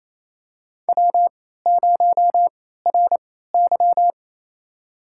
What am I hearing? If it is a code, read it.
Morse code, heard as a keyed tone, about 27 wpm, spaced as for 16 wpm, 700 Hz. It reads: W0RY
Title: USA Call Signs - 3x Character Spacing - 28wpm